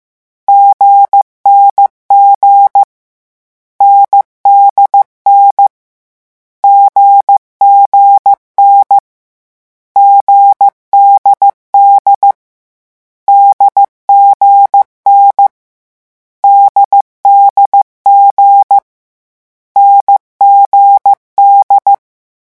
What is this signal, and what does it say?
Morse code, heard as a keyed tone, about 15 wpm, 800 Hz. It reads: GNG NDN GGN GDD DGN DDG NGD
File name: morse mp3 / lecon4-vitesse 14